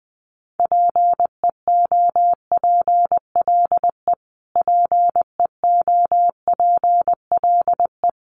Morse code, heard as a keyed tone, about 20 wpm, 700 Hz. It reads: PEOPLE PEOPLE
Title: Top 200 Words Repeated 20wpm